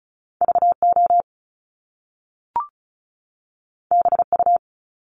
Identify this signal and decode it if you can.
Morse code, heard as a keyed tone, about 35 wpm, 700 Hz. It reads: VO BU
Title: Top Two Letter Combinations in Words 35wpm